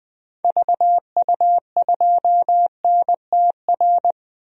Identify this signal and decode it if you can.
Morse code, heard as a keyed tone, about 21 wpm, 700 Hz. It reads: VU2NTR
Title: DXCC-100 Call Signs - 6 Characters - 20wpm